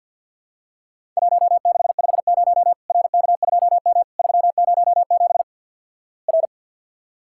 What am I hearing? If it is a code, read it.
Morse code, heard as a keyed tone, about 50 wpm, 700 Hz. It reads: 1650 RK2M 307 R
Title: Worked All Europe DX Contest - QTC Exchange - 50wpm